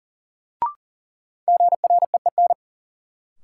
Morse code, heard as a keyed tone, about 40 wpm, 700 Hz. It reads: GREEN
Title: Sentences from Top 500 Words - New Word Drill - 40wpm